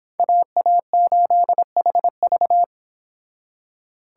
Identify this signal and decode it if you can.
Morse code, heard as a keyed tone, about 26 wpm, 700 Hz. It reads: AA8HV